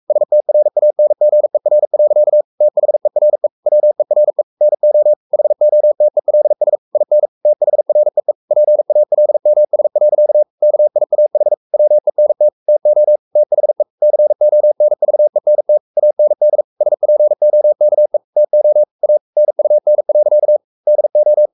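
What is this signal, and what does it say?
Morse code, heard as a keyed tone, about 44 wpm, 600 Hz. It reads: STRANGER. THERE WERE NO HOTELS IN THREE PALMS. KIAH WENT TO THE CONVENT AND SPOKE TO A NUN. DO
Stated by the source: Morse code at 43 wpm sending Ken Follett's novel Never in English, from CW med Gnister.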